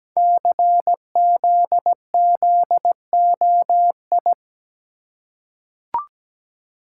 Morse code, heard as a keyed tone, about 17 wpm, 700 Hz. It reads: CZZOI
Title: Code Groups - 5 Letters - 17wpm